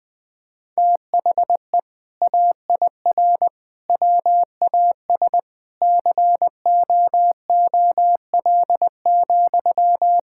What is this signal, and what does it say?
Morse code, heard as a keyed tone, about 20 wpm, 700 Hz. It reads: THE AIR WAS COOL,
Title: Sentences from Top 1600 Words 20wpm